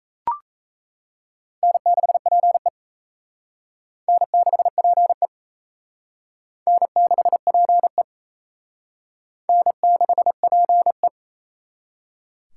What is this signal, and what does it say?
Morse code, heard as a keyed tone, about 34 wpm, 700 Hz. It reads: N6PE N6PE N6PE N6PE
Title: Speed Racing - USA 1x2 Call Signs 28wpm